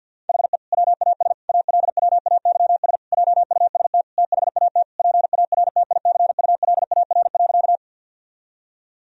Morse code, heard as a keyed tone, about 52 wpm, 700 Hz. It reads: HE WAS ALWAYS JUST THAT PARTICULAR.
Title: Sentences from Top 1000 Words - Review Part 4 - 50wpm